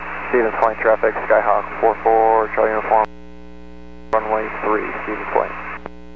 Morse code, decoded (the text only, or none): none